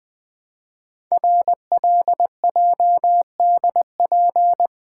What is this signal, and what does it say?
Morse code, heard as a keyed tone, about 20 wpm, 700 Hz. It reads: RLJDP
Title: Code Groups - 5 Letters - 20wpm